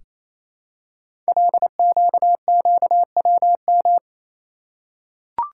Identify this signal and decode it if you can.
Morse code, heard as a keyed tone, about 27 wpm, 700 Hz. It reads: LQQWM
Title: Code Groups - 5 Letters - 28wpm